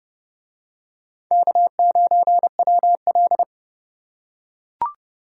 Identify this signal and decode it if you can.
Morse code, heard as a keyed tone, about 28 wpm, 700 Hz. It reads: K9WL